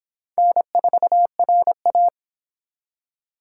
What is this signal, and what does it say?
Morse code, heard as a keyed tone, about 26 wpm, 700 Hz. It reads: N4RA